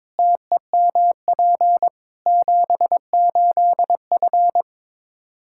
Morse code, heard as a keyed tone, about 22 wpm, 700 Hz. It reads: TEMP 78F
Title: Practice QSO - Exchange 2 - Info - 22wpm